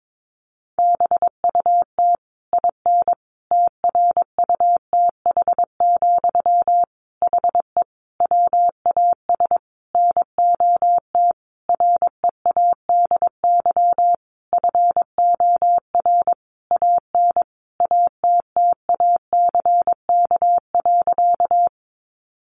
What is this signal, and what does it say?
Morse code, heard as a keyed tone, about 22 wpm, 700 Hz. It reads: BUT IN TRUTH, HE WAS NOT READY FOR AN ATTACK.
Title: Sentences from Top 1500 Words 22wpm